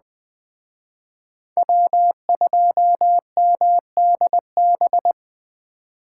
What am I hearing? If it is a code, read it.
Morse code, heard as a keyed tone, about 20 wpm, 700 Hz. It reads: W2MDB
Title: Speed Racing - USA 1x3 Call Signs 17wpm